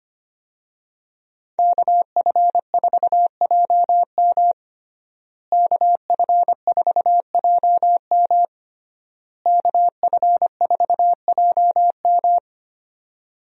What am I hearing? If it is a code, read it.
Morse code, heard as a keyed tone, about 25 wpm, 700 Hz. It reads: KF4JM KF4JM KF4JM
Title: USA Call Signs - 5 Characters - Repeated 3x - 25wpm